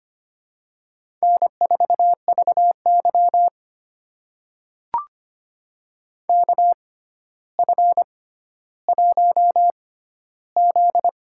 Morse code, heard as a keyed tone, about 25 wpm, spaced as for 13 wpm, 700 Hz. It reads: N4VY KF1Z
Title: USA Call Signs - 4 Characters - 6x Character Spacing - 25wpm